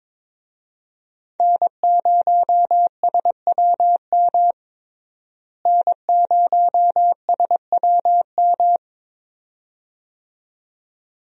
Morse code, heard as a keyed tone, about 22 wpm, 700 Hz. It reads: N0SWM N0SWM
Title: USA Call Signs - 5 Characters - Repeated 2x - 22wpm